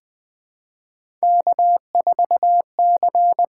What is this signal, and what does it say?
Morse code, heard as a keyed tone, about 20 wpm, 700 Hz. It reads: K4C
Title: USA Call Signs - 5 Characters - 6x Character Spacing - 20wpm